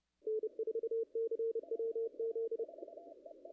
Morse code, heard as a keyed tone, about 30 wpm, 440 Hz. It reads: N4CWZ
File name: contests / ss-cw-k5tr-op-n5rz -0000